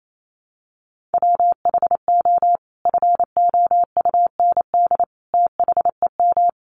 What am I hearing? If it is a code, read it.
Morse code, heard as a keyed tone, about 28 wpm, 700 Hz. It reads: WHO FOUND THEM
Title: Sentences from Top 300 Words 28wpm